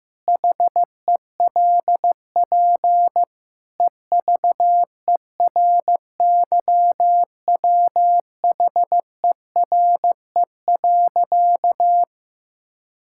Morse code, heard as a keyed tone, about 15 wpm, 700 Hz. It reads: HELP EVERYWHERE.
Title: Sentences from Top 1600 Words 15wpm